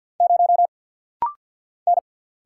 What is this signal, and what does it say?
Morse code, heard as a keyed tone, about 50 wpm, 700 Hz. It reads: 0 N
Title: Single Letter-Number Alt - Rapid-Fire 50wpm